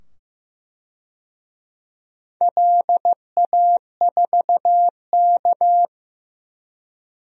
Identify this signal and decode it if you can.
Morse code, heard as a keyed tone, about 15 wpm, 700 Hz. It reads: LA4K
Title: DXCC-100 Call Signs - 4 Characters - 15wpm